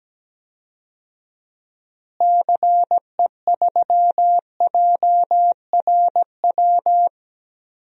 Morse code, heard as a keyed tone, about 17 wpm, 700 Hz. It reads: CE3JRW
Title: DXCC-100 Call Signs - 6 Characters - Repeated 2x - 17wpm